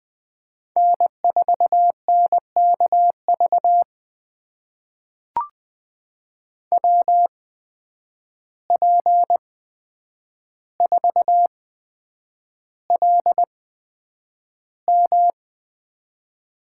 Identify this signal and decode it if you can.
Morse code, heard as a keyed tone, about 20 wpm, spaced as for 9 wpm, 700 Hz. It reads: N4NKV WP4LM
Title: USA Call Signs - 5 Characters - 8x Character Spacing - 20wpm